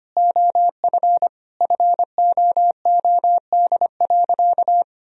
Morse code, heard as a keyed tone, about 25 wpm, 700 Hz. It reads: OF FOOD.
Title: Sentences from Top 1000 Words - Review Part 4 - 25wpm